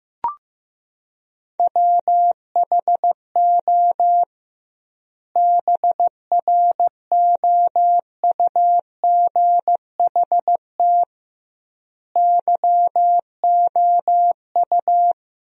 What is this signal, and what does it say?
Morse code, heard as a keyed tone, about 16 wpm, 700 Hz. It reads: WHO BROUGHT YOU
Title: 2x Word-Spacing - Sentences from Top 500 Words - 15wpm